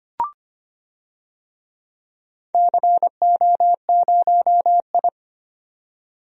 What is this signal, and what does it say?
Morse code, heard as a keyed tone, about 25 wpm, 700 Hz. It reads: CO0I
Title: DXCC-100 Call Signs - 4 Characters - 25wpm